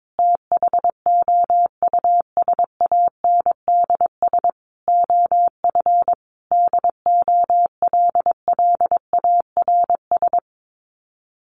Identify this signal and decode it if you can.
Morse code, heard as a keyed tone, about 22 wpm, 700 Hz. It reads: THOUSANDS OF DOLLARS
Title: Sets of 5 Words Encore2 22wpm